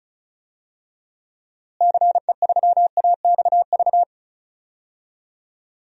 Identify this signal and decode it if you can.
Morse code, heard as a keyed tone, about 36 wpm, 700 Hz. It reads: CE3AXV
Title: DXCC-100 Call Signs - 6 Characters - Repeated 2x - 35wpm